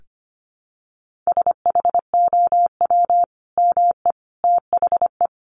Morse code, heard as a keyed tone, about 25 wpm, 700 Hz. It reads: SHOW ME THE
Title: Sentences from Top 1400 Words 25wpm